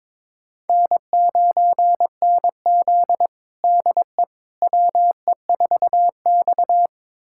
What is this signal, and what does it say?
Morse code, heard as a keyed tone, about 22 wpm, 700 Hz. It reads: N9NZ DE WE4X
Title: USA Callsign DE Callsign - 4 Characters - 22wpm